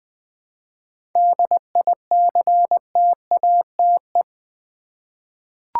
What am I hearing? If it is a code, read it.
Morse code, heard as a keyed tone, about 20 wpm, 700 Hz. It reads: DICTATE
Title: 7 Letter Words - First 3 Letters Given - 20wpm